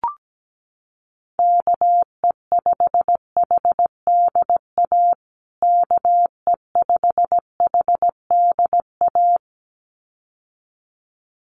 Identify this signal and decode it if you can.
Morse code, heard as a keyed tone, about 17 wpm, 700 Hz. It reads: KE5HDA KE5HDA
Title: USA Call Signs Repeated 2x 17wpm